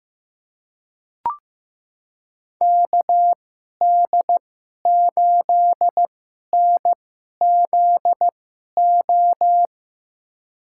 Morse code, heard as a keyed tone, about 15 wpm, spaced as for 11 wpm, 700 Hz. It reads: KD8NZO
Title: USA Call Signs - 6 Characters - 2x Character Spacing - 15wpm